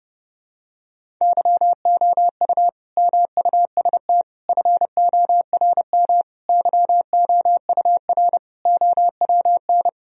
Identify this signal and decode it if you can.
Morse code, heard as a keyed tone, about 30 wpm, 700 Hz. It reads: YOU MUST FORM YOUR OWN
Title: Sentences from Top 1000 Words - Review Part 1 - 30wpm